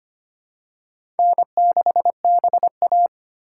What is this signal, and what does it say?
Morse code, heard as a keyed tone, about 25 wpm, 700 Hz. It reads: N6BA
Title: USA Call Signs - 4 Characters - 4x Character Spacing - 25wpm